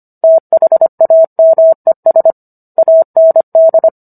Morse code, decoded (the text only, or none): THAMES AND